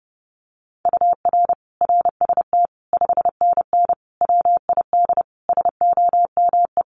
Morse code, heard as a keyed tone, about 30 wpm, 700 Hz. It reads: UR RST 5NN WID SOME
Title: Common QSO Phrases 30wpm